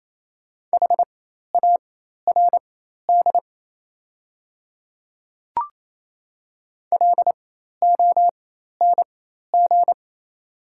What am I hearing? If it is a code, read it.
Morse code, heard as a keyed tone, about 27 wpm, spaced as for 13 wpm, 690 Hz. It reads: HARD LONG
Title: 4x Character Spacing - Top 300 Words - 28wpm